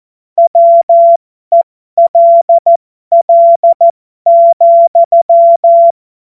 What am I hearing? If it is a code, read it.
Morse code, heard as a keyed tone, about 14 wpm, spaced as for 12 wpm, 680 Hz. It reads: WELL,
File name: Morsekurs / Witze